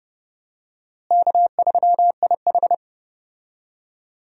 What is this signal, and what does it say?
Morse code, heard as a keyed tone, about 30 wpm, 700 Hz. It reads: K3IH
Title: USA Call Signs - 4 Characters - Repeated 3x - 30wpm